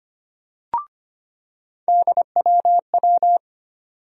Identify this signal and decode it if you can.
Morse code, heard as a keyed tone, about 25 wpm, 700 Hz. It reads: DWW
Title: USA Call Sign Postfixes - 3 Letters 25wpm